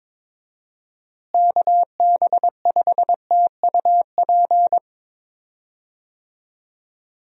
Morse code, heard as a keyed tone, about 22 wpm, 700 Hz. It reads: KB5TUP
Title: USA Call Signs - 6 Characters - Repeated 2x - 22wpm